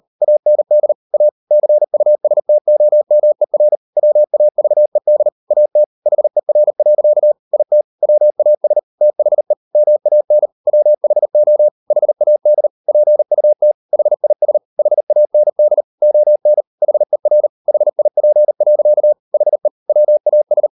Morse code, heard as a keyed tone, about 38 wpm, 600 Hz. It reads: AND A CUSTOMER WAVED AT HER. IT WAS THE MAN WHO HAD PUT HIS HAND ON HER HIP. HE WAS